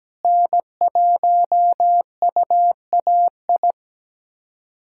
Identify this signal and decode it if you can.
Morse code, heard as a keyed tone, about 17 wpm, 700 Hz. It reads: N1UAI
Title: USA Callsign DE Callsign - 5 Characters - 17wpm